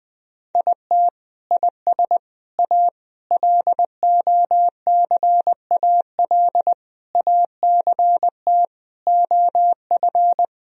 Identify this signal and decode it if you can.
Morse code, heard as a keyed tone, about 20 wpm, 700 Hz. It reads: IT IS A LOCAL ACT OF
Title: Sentences from Top 1000 Words - Review Part 3 - 20wpm